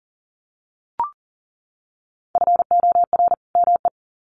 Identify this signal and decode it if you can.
Morse code, heard as a keyed tone, about 40 wpm, 700 Hz. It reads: FOR ME
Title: Sets of 2 Words 40wpm